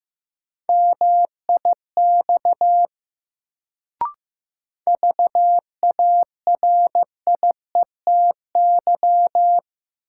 Morse code, heard as a keyed tone, about 15 wpm, 700 Hz. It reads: MIX VARIETY
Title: Sentences from Top 1000 Words - Review Part 4 - New Word Drill - 15wpm